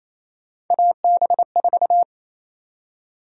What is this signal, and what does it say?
Morse code, heard as a keyed tone, about 28 wpm, 700 Hz. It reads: AB4
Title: USA Call Sign Prefixes - 28wpm